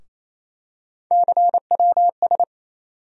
Morse code, heard as a keyed tone, about 27 wpm, 700 Hz. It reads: CWS
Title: USA Call Sign Postfixes - 3 Letters 28wpm